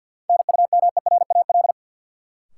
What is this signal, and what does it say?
Morse code, heard as a keyed tone, about 52 wpm, 700 Hz. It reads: NUMERAL
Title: Sentences from Top 1000 Words - Review Part 2 - New Word Drill - 50wpm